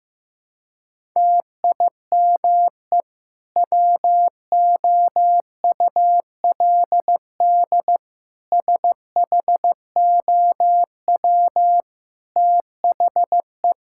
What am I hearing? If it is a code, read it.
Morse code, heard as a keyed tone, about 15 wpm, 700 Hz. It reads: TIME WOULD SHOW THE
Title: Sentences from Top 1000 Words - Review Part 5 - 15wpm